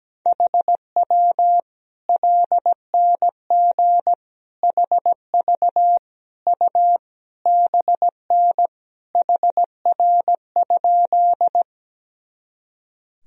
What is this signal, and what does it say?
Morse code, heard as a keyed tone, about 17 wpm, 700 Hz. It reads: HW LNG HV U BN HR?
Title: Sentences from Top 200 Words and Abbreviations 17wpm